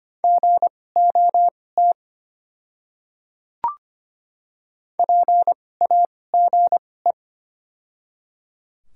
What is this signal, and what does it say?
Morse code, heard as a keyed tone, about 25 wpm, spaced as for 18 wpm, 700 Hz. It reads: GOT PAGE
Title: 2x Character Spacing - Top 300 Words - 25wpm